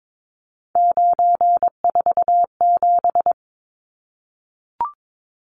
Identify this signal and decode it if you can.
Morse code, heard as a keyed tone, about 22 wpm, 700 Hz. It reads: 947
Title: Numbers - 3-Digit 22wpm